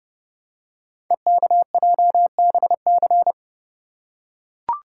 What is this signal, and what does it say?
Morse code, heard as a keyed tone, about 29 wpm, 700 Hz. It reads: EKJBC
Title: Code Groups - 5 Letters - 30wpm